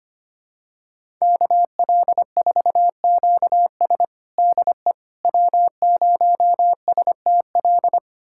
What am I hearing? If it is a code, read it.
Morse code, heard as a keyed tone, about 25 wpm, 700 Hz. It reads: KL4QS DE W0STL
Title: USA Callsign DE Callsign - 5 Characters - 25wpm